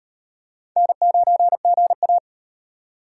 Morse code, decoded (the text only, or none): N9GA